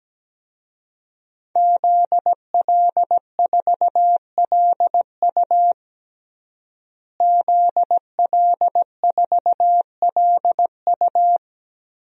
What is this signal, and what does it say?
Morse code, heard as a keyed tone, about 17 wpm, 700 Hz. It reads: ZL4LU ZL4LU
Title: DXCC-100 Call Signs - 5 Characters - Repeated 3x - 17wpm